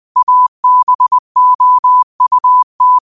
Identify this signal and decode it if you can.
Morse code, heard as a keyed tone, about 20 wpm, 1000 Hz. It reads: ABOUT
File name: qotdfiles / 20wpm